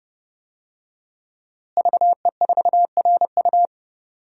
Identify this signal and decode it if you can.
Morse code, heard as a keyed tone, about 32 wpm, 700 Hz. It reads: VE4RU
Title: DXCC-100 Call Signs - 5 Characters - Repeated 3x - 30wpm